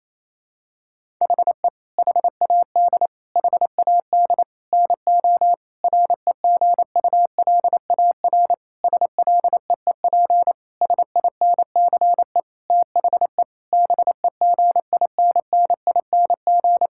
Morse code, heard as a keyed tone, about 28 wpm, 700 Hz. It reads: HE HAD HAD NO REGULAR SLEEP SINCE THE BEGINNING
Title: Sentences from Top 1000 Words - Review Part 6 - 28wpm